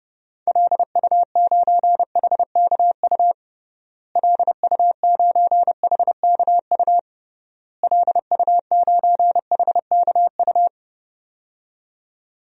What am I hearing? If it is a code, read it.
Morse code, heard as a keyed tone, about 30 wpm, 700 Hz. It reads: LU9HKU LU9HKU LU9HKU
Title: DXCC-100 Call Signs - 6 Characters - Repeated 3x - 30wpm